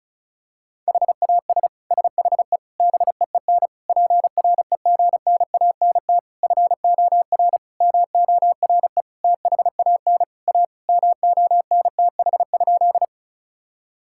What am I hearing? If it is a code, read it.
Morse code, heard as a keyed tone, about 36 wpm, 700 Hz. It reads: HAS SHE BEEN PREGNANT FOR MORE THAN A MONTH?
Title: Sentences from Top 1000 Words - Review Part 6 - 35wpm